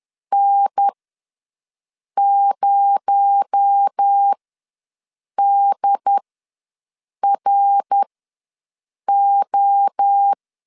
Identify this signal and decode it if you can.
Morse code, heard as a keyed tone, about 10 wpm, spaced as for 6 wpm, 800 Hz. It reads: N0DRO
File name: LEZIONI / 7 GF9Z / 7 G ws 10wpm 3843